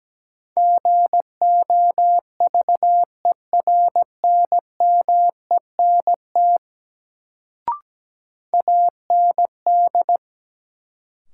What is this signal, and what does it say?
Morse code, heard as a keyed tone, about 17 wpm, 700 Hz. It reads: GOVERNMENT AND
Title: Sets of 2 Words - New Word Drill - 17wpm